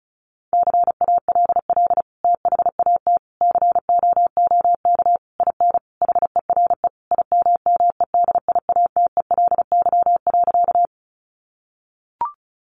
Morse code, heard as a keyed tone, about 34 wpm, 700 Hz. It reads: CALL THAT COOK IN HERE IMMEDIATELY.
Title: Sentences from Top 1100 Words 35wpm